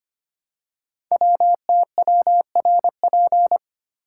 Morse code, heard as a keyed tone, about 25 wpm, 700 Hz. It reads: WTWRP
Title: Code Groups - 5 Letters - 25wpm